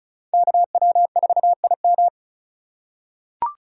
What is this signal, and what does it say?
Morse code, heard as a keyed tone, about 36 wpm, 700 Hz. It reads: KW4IM
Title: USA Call Signs - 5 Characters - 4x Character Spacing - 35wpm